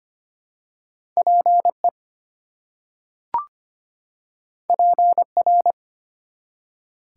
Morse code, heard as a keyed tone, about 25 wpm, 700 Hz. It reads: PE PR
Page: morse-code-ninja-practice-25wpm